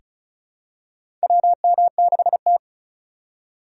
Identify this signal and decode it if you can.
Morse code, heard as a keyed tone, about 34 wpm, 700 Hz. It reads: WM6T
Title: USA Call Signs - 3x Character Spacing - 35wpm